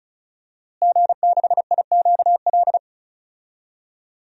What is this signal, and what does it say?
Morse code, heard as a keyed tone, about 34 wpm, 700 Hz. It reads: G6IQL
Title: DXCC-100 Call Signs - 5 Characters - Repeated 2x - 35wpm